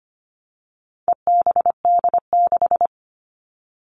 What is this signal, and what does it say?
Morse code, heard as a keyed tone, about 25 wpm, 700 Hz. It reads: EBD6
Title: All Cut Numbers - 4-Digit 25wpm